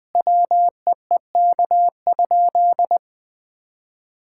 Morse code, heard as a keyed tone, about 21 wpm, 700 Hz. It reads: WEEK?
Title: Sentences from Top 1000 Words - Review Part 6 - 20wpm